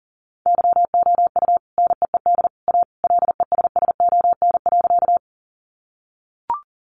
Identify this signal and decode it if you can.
Morse code, heard as a keyed tone, about 40 wpm, 700 Hz. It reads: YOU NEED A LESSON.
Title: Sentences from Top 1200 Words 40wpm